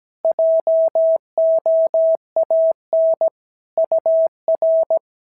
Morse code, heard as a keyed tone, about 17 wpm, 650 Hz. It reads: JOAN UR
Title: SOTA QSO Practice 17wpm